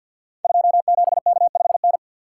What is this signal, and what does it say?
Morse code, heard as a keyed tone, about 48 wpm, 700 Hz. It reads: 27K5N
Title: Code Groups - 5 Letter:Numbers - 50wpm